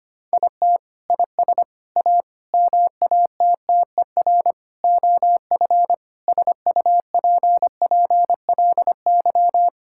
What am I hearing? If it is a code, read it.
Morse code, heard as a keyed tone, about 25 wpm, 700 Hz. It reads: IT IS A MATTER OF SUPPLY
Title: Sentences from Top 1000 Words - Review Part 5 - 25wpm